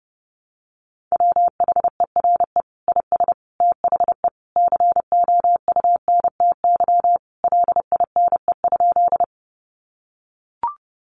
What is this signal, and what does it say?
Morse code, heard as a keyed tone, about 30 wpm, 700 Hz. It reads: WHERE IS THE COUNTY LINE?
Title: Sentences from Top 1200 Words 30wpm